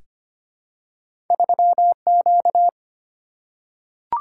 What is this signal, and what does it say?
Morse code, heard as a keyed tone, about 25 wpm, 700 Hz. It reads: 3 Q
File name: Two Letter-Number Combinations - 25wpm